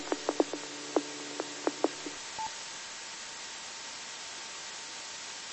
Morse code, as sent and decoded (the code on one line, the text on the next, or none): none